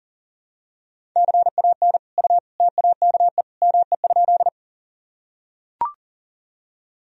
Morse code, heard as a keyed tone, about 40 wpm, 700 Hz. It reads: CAN U TAKE ME?
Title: Sentences from Top 200 Words and Abbreviations 40wpm